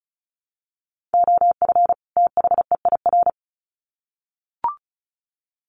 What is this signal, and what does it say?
Morse code, heard as a keyed tone, about 34 wpm, 700 Hz. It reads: OF THEIR
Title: Sets of 2 Words 35wpm